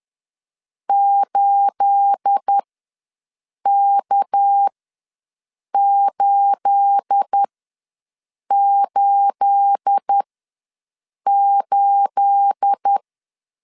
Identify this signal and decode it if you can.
Morse code, heard as a keyed tone, about 10 wpm, spaced as for 6 wpm, 800 Hz. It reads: 8K888
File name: LEZIONI / 9 28 ws 10wpm 4124